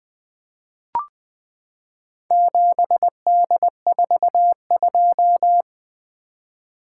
Morse code, heard as a keyed tone, about 20 wpm, 700 Hz. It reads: 7D42
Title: Four Letter-Number Combinations - 20wpm